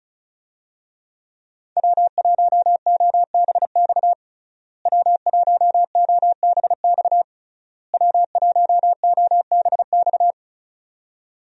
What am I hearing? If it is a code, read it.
Morse code, heard as a keyed tone, about 36 wpm, 700 Hz. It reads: W1OBX W1OBX W1OBX
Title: USA Call Signs - 5 Characters - Repeated 3x - 35wpm